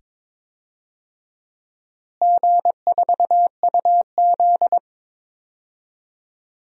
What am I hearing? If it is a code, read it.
Morse code, heard as a keyed tone, about 22 wpm, 700 Hz. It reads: G4UZ
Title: DXCC-100 Call Signs - 4 Characters - Repeated 2x - 22wpm